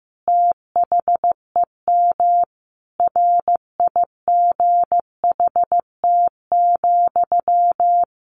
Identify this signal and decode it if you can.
Morse code, heard as a keyed tone, about 15 wpm, 700 Hz. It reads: THEM RIGHT,
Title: A Sound Of Thunder 15wpm